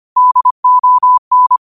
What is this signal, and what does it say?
Morse code, heard as a keyed tone, about 25 wpm, 1000 Hz. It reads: DON
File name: qotdfiles / 25wpm